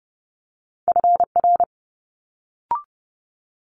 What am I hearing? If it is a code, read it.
Morse code, heard as a keyed tone, about 30 wpm, 700 Hz. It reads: FR
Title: Multi-letter Phonemes Quick Fire 30wpm